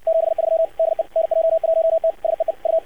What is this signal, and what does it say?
Morse code, heard as a keyed tone, about 54 wpm, 650 Hz. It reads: 73 DE NJ1T SEE U
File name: archived / CRSnet